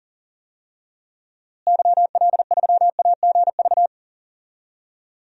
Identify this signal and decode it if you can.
Morse code, heard as a keyed tone, about 40 wpm, 700 Hz. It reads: YL3AGV
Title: DXCC-100 Call Signs - 6 Characters - Repeated 2x - 40wpm